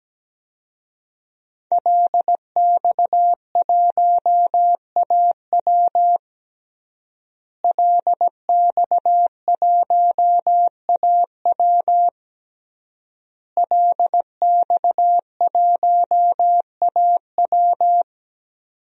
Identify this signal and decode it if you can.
Morse code, heard as a keyed tone, about 17 wpm, 700 Hz. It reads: LX1AW LX1AW LX1AW
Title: DXCC-100 Call Signs - 5 Characters - Repeated 3x - 17wpm